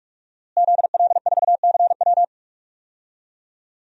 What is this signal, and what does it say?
Morse code, heard as a keyed tone, about 44 wpm, 700 Hz. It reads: ZL4CW
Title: DXCC-100 Call Signs - 5 Characters - Repeated 2x - 45wpm